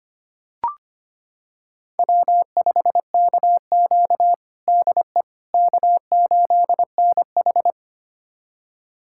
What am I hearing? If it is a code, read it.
Morse code, heard as a keyed tone, about 25 wpm, 700 Hz. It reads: W5KQ DE K8NH
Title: USA Callsign DE Callsign - 4 Characters - 25wpm